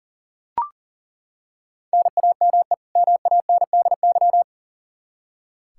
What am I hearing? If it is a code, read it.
Morse code, heard as a keyed tone, about 39 wpm, 700 Hz. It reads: NAME MANDY
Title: Practice QSO - Exchange 1 - Beginning - 40wpm